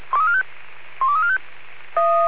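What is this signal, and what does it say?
Five-tone selective call (ZVEI), heard as quick runs of five tones, tones of about 70 ms. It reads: 12345
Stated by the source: ZVEI five-tone call 12345 received on SDR.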